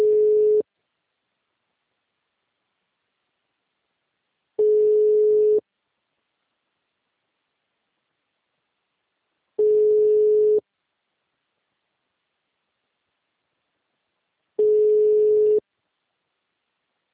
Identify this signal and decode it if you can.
Ringback tone (Europe, 425 Hz).
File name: out- -2033- - - .76084